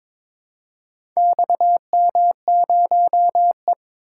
Morse code, heard as a keyed tone, about 22 wpm, 700 Hz. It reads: XM0E
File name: practice / 22wpm / Four Letter-Number Combinations - 22wpm